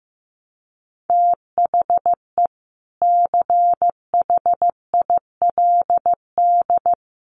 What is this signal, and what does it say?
Morse code, heard as a keyed tone, about 14 wpm, 700 Hz. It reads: THE CHILD
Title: Sentences from Top 800 Words 15wpm